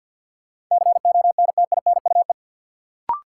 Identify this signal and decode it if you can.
Morse code, heard as a keyed tone, about 53 wpm, 700 Hz. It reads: CONTINUE
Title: Sentences from Top 1100 Words - New Word Drill Alt - 50wpm